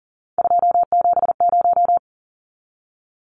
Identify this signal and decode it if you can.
Morse code, heard as a keyed tone, about 40 wpm, 700 Hz. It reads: 270
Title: Numbers - 3-Digit 40wpm